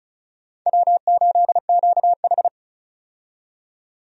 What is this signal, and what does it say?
Morse code, heard as a keyed tone, about 35 wpm, 700 Hz. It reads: W8QH
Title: USA Call Signs - 4 Characters - 35wpm